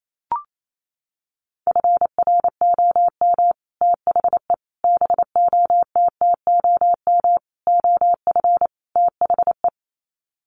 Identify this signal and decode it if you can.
Morse code, heard as a keyed tone, about 28 wpm, 700 Hz. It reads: FROM THE BOTTOM OF THE
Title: Sets of 5 Words Encore2 28wpm